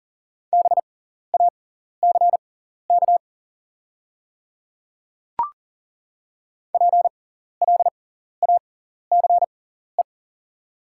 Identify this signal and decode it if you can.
Morse code, heard as a keyed tone, about 38 wpm, spaced as for 14 wpm, 690 Hz. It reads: BACK PLACE
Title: 6x Character Spacing - Top 200 Words - 40wpm